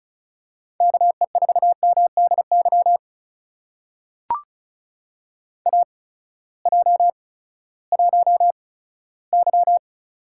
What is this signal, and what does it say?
Morse code, heard as a keyed tone, about 36 wpm, spaced as for 18 wpm, 700 Hz. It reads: KE4MDY AJ1Y
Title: USA Call Signs - 8x Character Spacing - 35wpm